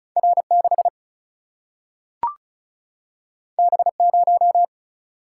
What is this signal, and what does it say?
Morse code, heard as a keyed tone, about 36 wpm, 700 Hz. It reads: R 6 B 0
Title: Single-Letter Single-Number 35wpm